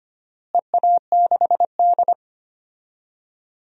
Morse code, heard as a keyed tone, about 25 wpm, 700 Hz. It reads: EA6D
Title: DXCC-100 Call Signs - 4 Characters - Repeated 3x - 25wpm